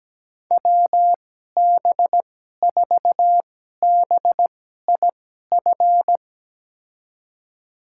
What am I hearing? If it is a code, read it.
Morse code, heard as a keyed tone, about 17 wpm, spaced as for 12 wpm, 700 Hz. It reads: WB4BIF
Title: USA Call Signs - 6 Characters - 2x Character Spacing - 17wpm